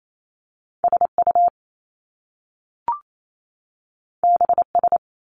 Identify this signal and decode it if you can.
Morse code, heard as a keyed tone, about 28 wpm, 700 Hz. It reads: SU BS
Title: Top Two Letter Combinations in Words 28wpm